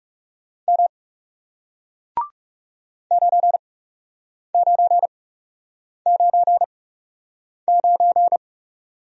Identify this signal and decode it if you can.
Morse code, heard as a keyed tone, about 40 wpm, spaced as for 8 wpm, 700 Hz. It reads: M 9999